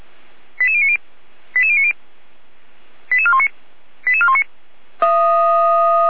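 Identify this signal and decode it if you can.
Five-tone selective call (ZVEI), heard as quick runs of five tones, tones of about 70 ms. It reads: 88090 88090 88419 88419
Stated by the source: ZVEI five-tone call received on an SDR, from the Feuerwehr Orlamünde fire brigade.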